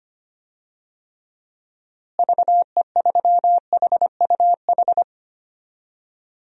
Morse code, heard as a keyed tone, about 25 wpm, 700 Hz. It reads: VE3HUH
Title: DXCC-100 Call Signs - 6 Characters - Repeated 3x - 25wpm